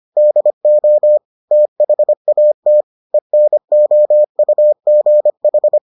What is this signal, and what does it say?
Morse code, heard as a keyed tone, about 25 wpm, 600 Hz. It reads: DO THAT ENOUGH